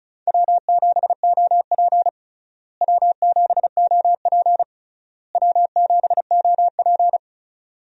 Morse code, heard as a keyed tone, about 36 wpm, 700 Hz. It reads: W7OP W7OP W7OP
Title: USA Call Signs - 4 Characters - Repeated 3x - 35wpm